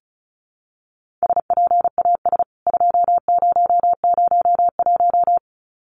Morse code, heard as a keyed tone, about 34 wpm, 700 Hz. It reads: SPAS 2001
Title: CWT Exchange 35wpm